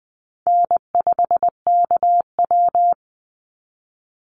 Morse code, heard as a keyed tone, about 20 wpm, 700 Hz. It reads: N5KW
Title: Call Signs Repeated 3x - 20wpm